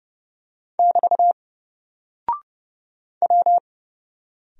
Morse code, heard as a keyed tone, about 30 wpm, 700 Hz. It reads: <BT> W
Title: Single Character (Letter:Number:Punctuation:Prosign) Alt 30wpm